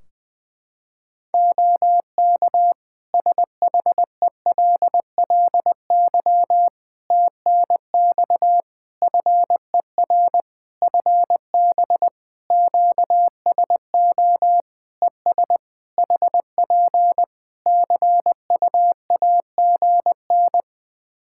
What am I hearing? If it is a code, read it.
Morse code, heard as a keyed tone, about 20 wpm, 700 Hz. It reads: OK SHELLY TNX FER FB QSO ES HP CUAGN
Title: Practice QSO - Exchange 4 - Ending - 20wpm